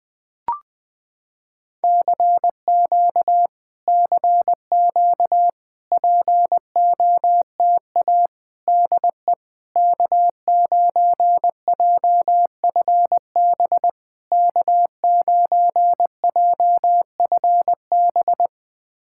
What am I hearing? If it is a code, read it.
Morse code, heard as a keyed tone, about 20 wpm, 700 Hz. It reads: CQ CQ POTA DE K9JFB K9JFB
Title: POTA QSO Practice - 20wpm